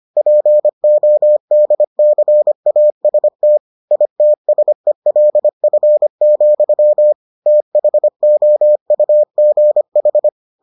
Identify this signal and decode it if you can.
Morse code, heard as a keyed tone, about 25 wpm, 600 Hz. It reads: PODCAST ITSELF, THOUGH